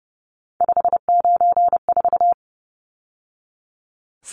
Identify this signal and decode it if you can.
Morse code, heard as a keyed tone, about 30 wpm, 700 Hz. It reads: 594
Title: Numbers - 3-Digit 30wpm